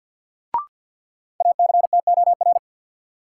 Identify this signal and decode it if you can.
Morse code, heard as a keyed tone, about 53 wpm, 700 Hz. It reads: ACTOR
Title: Sentences from Top 1500 Words - New Word Drill - 50wpm